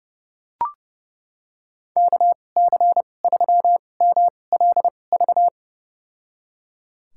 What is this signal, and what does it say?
Morse code, heard as a keyed tone, about 31 wpm, spaced as for 22 wpm, 700 Hz. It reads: KC3MLV
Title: USA Call Signs - 6 Characters - 2x Character Spacing - 30wpm